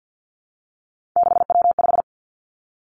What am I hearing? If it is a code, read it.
Morse code, heard as a keyed tone, about 50 wpm, 700 Hz. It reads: 6W5
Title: Three Challenging Letter-Number Combinations 50wpm